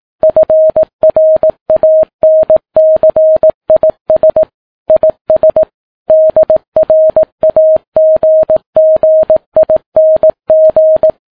Morse code, heard as a keyed tone, about 18 wpm, 630 Hz. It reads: FRANCIS IS DRAGGING
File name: Ham Radio Sat